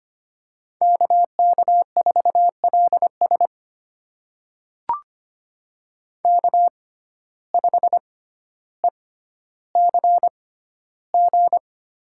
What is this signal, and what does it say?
Morse code, heard as a keyed tone, about 25 wpm, spaced as for 13 wpm, 700 Hz. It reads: KK4LS K5ECG
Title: USA Call Signs - 5 Characters - 6x Character Spacing - 25wpm